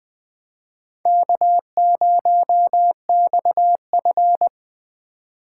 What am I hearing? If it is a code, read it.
Morse code, heard as a keyed tone, about 21 wpm, 700 Hz. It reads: K0XF
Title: Speed Racing - USA 1x2 Call Signs 17wpm